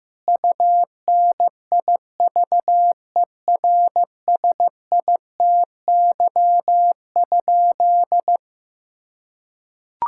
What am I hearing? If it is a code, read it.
Morse code, heard as a keyed tone, about 15 wpm, 700 Hz. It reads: UNIVERSITY?
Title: Sentences from Top 1000 Words - Review Part 5 - 15wpm